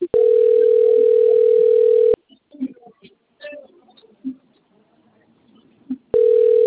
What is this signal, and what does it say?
Ringback tone (North America), 440 + 480 Hz.
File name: rg-1600-unknown- - - .24670